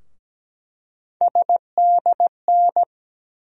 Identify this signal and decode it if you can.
Morse code, heard as a keyed tone, about 17 wpm, 700 Hz. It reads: SDN
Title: USA Call Sign Postfixes - 3 Letters 17wpm